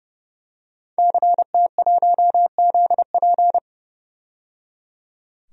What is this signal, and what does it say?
Morse code, heard as a keyed tone, about 30 wpm, 700 Hz. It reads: CT1ZP
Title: DXCC-100 Call Signs - 5 Characters - 30wpm